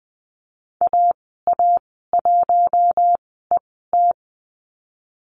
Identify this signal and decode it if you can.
Morse code, heard as a keyed tone, about 20 wpm, spaced as for 14 wpm, 700 Hz. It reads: AA1ET
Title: 055 Lesson 10g - Letter r - Call Signs 20wpm